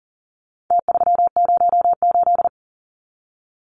Morse code, heard as a keyed tone, about 40 wpm, 700 Hz. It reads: T308
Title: Numbers - Contest Style - 4-Digit 40wpm